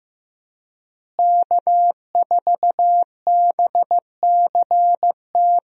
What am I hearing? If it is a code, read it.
Morse code, heard as a keyed tone, about 15 wpm, 700 Hz. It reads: K4BCT
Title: USA Call Signs - 5 Characters - Repeated 3x - 15wpm